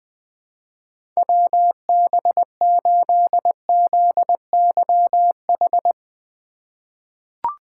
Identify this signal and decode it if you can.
Morse code, heard as a keyed tone, about 20 wpm, 700 Hz. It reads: WB8ZYH
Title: USA Call Signs - 6 Characters - 3x Character Spacing - 20wpm